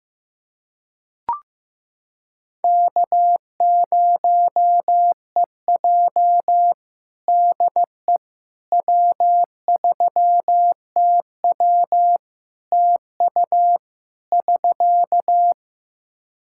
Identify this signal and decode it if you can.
Morse code, heard as a keyed tone, about 15 wpm, 700 Hz. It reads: K0EJ DE W3TW TU <SK>
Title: Practice QSO - Exchange 4 - Ending - 15wpm